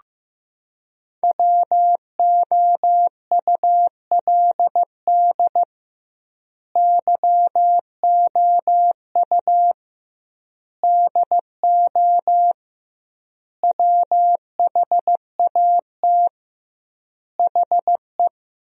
Morse code, heard as a keyed tone, about 16 wpm, 700 Hz. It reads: WOULD YOU DO WHAT HE
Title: 2x Word-Spacing - Sentences from Top 100 Words - 15wpm